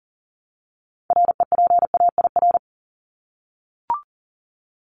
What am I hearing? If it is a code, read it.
Morse code, heard as a keyed tone, about 40 wpm, 700 Hz. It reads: REPAIR
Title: Top 1501-1750 Words 40wpm